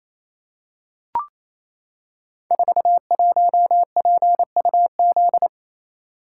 Morse code, heard as a keyed tone, about 27 wpm, 700 Hz. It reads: 41PUZ
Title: Code Groups - 5 Letter:Numbers - 28wpm